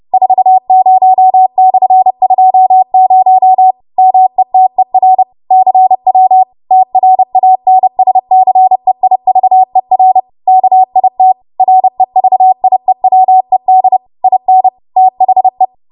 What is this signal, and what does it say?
Morse code, heard as a keyed tone, about 30 wpm, 750 Hz. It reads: 40/20 METER CW TRANSCEIVER KIT REVIEWED IN THE